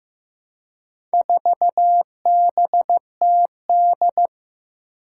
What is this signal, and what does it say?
Morse code, heard as a keyed tone, about 15 wpm, 700 Hz. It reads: 4BTD
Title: Four Challenging Letter-Number Combinations 15wpm